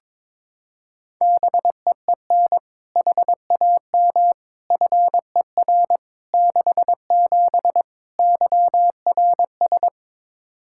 Morse code, heard as a keyed tone, about 22 wpm, 700 Hz. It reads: BEEN HAM FER 67 YRS
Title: Practice QSO - Exchange 3 - More Info - 22wpm